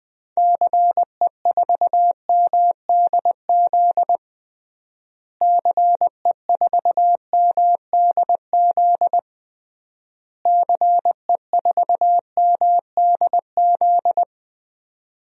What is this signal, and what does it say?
Morse code, heard as a keyed tone, about 20 wpm, 700 Hz. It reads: CE4MDZ CE4MDZ CE4MDZ
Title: DXCC-100 Call Signs - 6 Characters - Repeated 3x - 20wpm